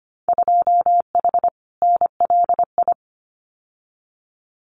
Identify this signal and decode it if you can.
Morse code, heard as a keyed tone, about 25 wpm, 700 Hz. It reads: 2H NLI
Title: Winter Field Day Exchanges Encore 25wpm